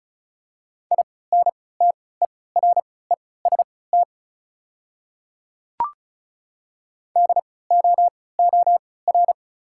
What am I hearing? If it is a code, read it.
Morse code, heard as a keyed tone, about 36 wpm, spaced as for 20 wpm, 690 Hz. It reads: INTEREST DOOR
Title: 3x Character Spacing - Top 500 Words - 35wpm